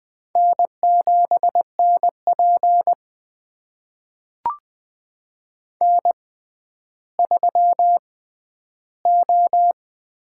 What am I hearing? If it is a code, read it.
Morse code, heard as a keyed tone, about 20 wpm, spaced as for 11 wpm, 700 Hz. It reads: N7NP N3O
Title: USA Call Signs - 6x Character Spacing - 20wpm